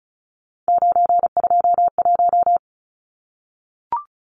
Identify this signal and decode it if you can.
Morse code, heard as a keyed tone, about 34 wpm, 700 Hz. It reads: 921
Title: Numbers - 3-Digit 35wpm